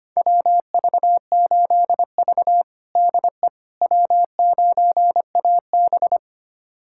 Morse code, heard as a keyed tone, about 25 wpm, 700 Hz. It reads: WV8V DE W9AB
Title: USA Callsign DE Callsign - 4 Characters - 25wpm